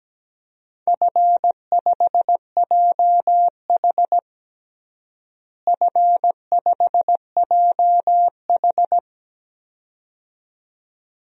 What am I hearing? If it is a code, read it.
Morse code, heard as a keyed tone, about 17 wpm, 700 Hz. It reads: F5JH F5JH
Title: DXCC-100 Call Signs - 4 Characters - Repeated 2x - 17wpm